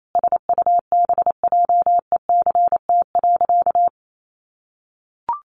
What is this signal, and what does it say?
Morse code, heard as a keyed tone, about 28 wpm, 700 Hz. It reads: SUBJECT.
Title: Sentences from Top 1400 Words 28wpm